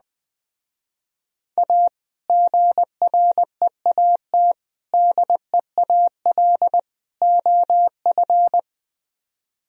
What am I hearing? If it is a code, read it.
Morse code, heard as a keyed tone, about 20 wpm, 700 Hz. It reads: A GREAT DEAL OF